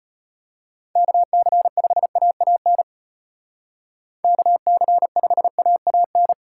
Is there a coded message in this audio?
Yes: Morse code from 1.0 to 6.3 s, heard as a keyed tone, about 36 wpm, 700 Hz. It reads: KC5AAN KC5AAN